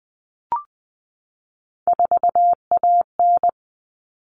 Morse code, heard as a keyed tone, about 20 wpm, 700 Hz. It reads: 4AN
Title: All Cut Numbers - 3-Digit 20wpm